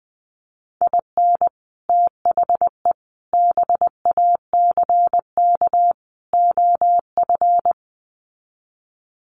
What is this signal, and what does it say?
Morse code, heard as a keyed tone, about 19 wpm, 700 Hz. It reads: IN THE BACK OF
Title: Sets of 4 Words 20wpm